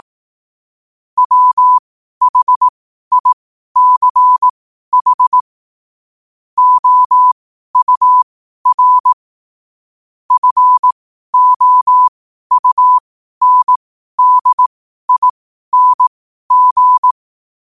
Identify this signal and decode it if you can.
Morse code, heard as a keyed tone, about 18 wpm, spaced as for 12 wpm, 1000 Hz. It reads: WHICH OUR FOUNDING
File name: qotdfiles / 12wpm